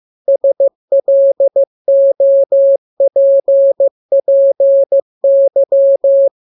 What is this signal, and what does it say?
Morse code, heard as a keyed tone, about 15 wpm, 550 Hz. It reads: SLOPPY